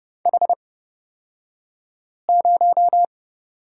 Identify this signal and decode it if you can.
Morse code, heard as a keyed tone, about 30 wpm, spaced as for 5 wpm, 700 Hz. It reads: H0
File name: Lesson 37d - Letter x - Two Characters 30wpm